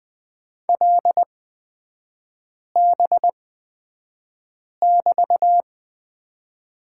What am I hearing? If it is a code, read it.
Morse code, heard as a keyed tone, about 20 wpm, spaced as for 5 wpm, 700 Hz. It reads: LB<BT>
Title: 224 Lesson 36e - BT - Three Characters 20wpm